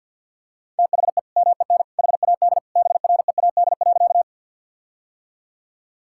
Morse code, heard as a keyed tone, about 50 wpm, 700 Hz. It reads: THE MEN HAD BREAD.
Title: Sentences from Top 1600 Words 50wpm